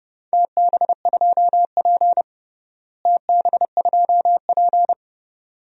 Morse code, heard as a keyed tone, about 30 wpm, 700 Hz. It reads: TB2P TB2P
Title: DXCC-100 Call Signs - 4 Characters - Repeated 2x - 30wpm